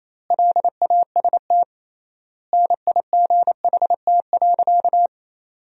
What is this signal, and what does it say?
Morse code, heard as a keyed tone, about 27 wpm, 700 Hz. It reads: LAST NIGHT.
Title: 3x Word-Spacing - Sentences from Top 300 Words - 28wpm